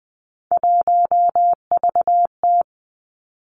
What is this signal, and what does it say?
Morse code, heard as a keyed tone, about 20 wpm, 700 Hz. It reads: 1VT
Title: Three Challenging Letter-Number Combinations 20wpm